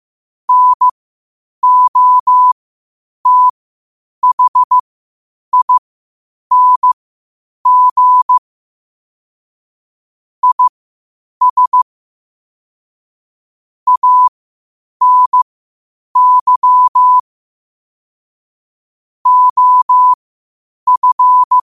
Morse code, heard as a keyed tone, about 15 wpm, spaced as for 8 wpm, 1000 Hz. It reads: NOTHING IS ANY OF
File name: qotdfiles / 8wpm